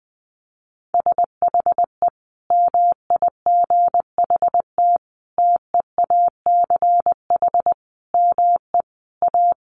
Morse code, heard as a keyed tone, about 20 wpm, 700 Hz. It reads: SHE MIGHT TEACH ME A
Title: Sentences from Top 1200 Words 20wpm